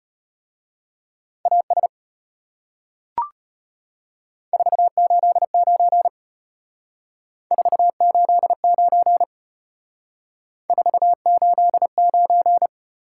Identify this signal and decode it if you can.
Morse code, heard as a keyed tone, about 33 wpm, 700 Hz. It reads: AS 489 489 489